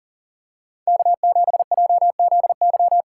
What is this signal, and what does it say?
Morse code, heard as a keyed tone, about 40 wpm, 700 Hz. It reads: K7JZY
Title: USA Call Signs - 5 Characters - 2x Character Spacing - 40wpm